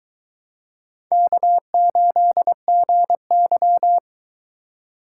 Morse code, heard as a keyed tone, about 24 wpm, 700 Hz. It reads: K8GY